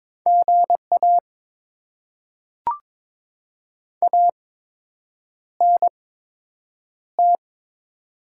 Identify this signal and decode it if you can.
Morse code, heard as a keyed tone, about 22 wpm, spaced as for 8 wpm, 700 Hz. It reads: GA ANT
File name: Top 20 QSO Elements - 8x Character Spacing - 22wpm